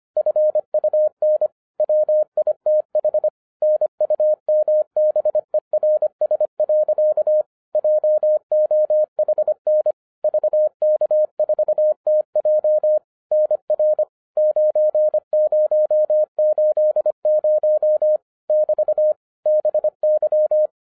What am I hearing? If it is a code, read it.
Morse code, heard as a keyed tone, about 25 wpm, 610 Hz. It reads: FUN WITH NUMBERS. JOHN VK4TJ NR 9080 <BT> BY
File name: 2012-01-mp3 / n14-FUN WITH NUMBERS-25wpm